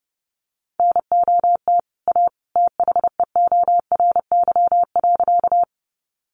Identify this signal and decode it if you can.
Morse code, heard as a keyed tone, about 30 wpm, 700 Hz. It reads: NOT A THEORY.
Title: Sentences from Top 600 Words 30wpm